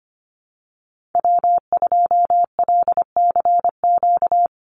Morse code, heard as a keyed tone, about 25 wpm, 700 Hz. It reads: W2LCQ
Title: Call Signs Repeated 2x - 25wpm